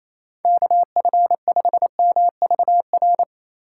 Morse code, heard as a keyed tone, about 28 wpm, 700 Hz. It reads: KF5MVR
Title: USA Call Signs - 6 Characters - 2x Character Spacing - 28wpm